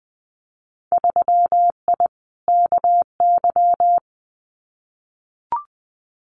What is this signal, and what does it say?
Morse code, heard as a keyed tone, about 20 wpm, 700 Hz. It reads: 3I KY
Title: Winter Field Day Exchanges 20wpm